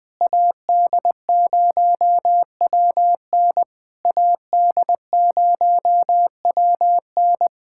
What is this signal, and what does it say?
Morse code, heard as a keyed tone, about 20 wpm, 700 Hz. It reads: AD0WN AD0WN
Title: POTA QSO Practice - 20wpm